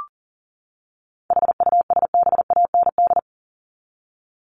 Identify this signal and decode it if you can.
Morse code, heard as a keyed tone, about 40 wpm, 700 Hz. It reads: HUSBAND
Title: Top 1501-1750 Words 40wpm